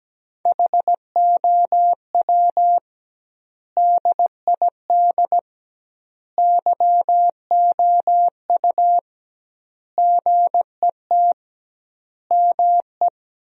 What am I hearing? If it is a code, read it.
Morse code, heard as a keyed tone, about 17 wpm, 700 Hz. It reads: HOW DID YOU GET ME
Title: 2x Word-Spacing - Sentences from Top 200 Words - 17wpm